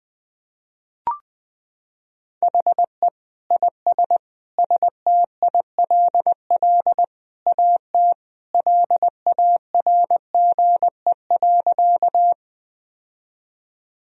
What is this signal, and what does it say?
Morse code, heard as a keyed tone, about 20 wpm, 700 Hz. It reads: HE IS STILL AT LARGE.
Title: Sentences from Top 1000 Words - Review Part 1 - 20wpm